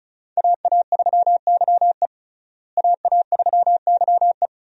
Morse code, heard as a keyed tone, about 34 wpm, 700 Hz. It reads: AA3YE AA3YE
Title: USA Call Signs - 5 Characters - Repeated 3x - 35wpm